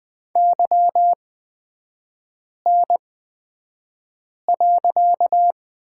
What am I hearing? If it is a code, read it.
Morse code, heard as a keyed tone, about 20 wpm, spaced as for 5 wpm, 690 Hz. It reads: YN.
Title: 255 Lesson 41e - Period - Three Characters 20wpm